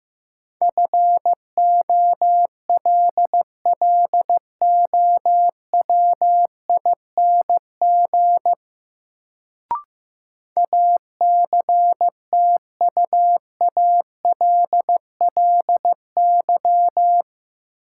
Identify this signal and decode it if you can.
Morse code, heard as a keyed tone, about 15 wpm, 700 Hz. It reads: FOLLOWING ACTUALLY
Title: Sentences from Top 700 Words - New Word Drill - 15wpm